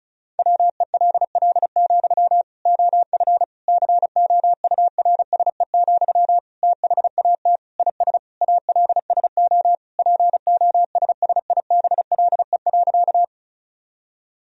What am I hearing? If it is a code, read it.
Morse code, heard as a keyed tone, about 34 wpm, 700 Hz. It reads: WELL, OF COURSE, THAT IS ALSO POSSIBLE.
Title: Sentences from Top 1000 Words - Review Part 1 - 35wpm